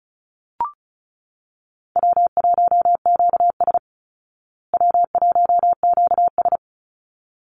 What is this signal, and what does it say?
Morse code, heard as a keyed tone, about 34 wpm, 700 Hz. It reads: W1QS W1QS
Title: Call Signs Repeated 3x - 35wpm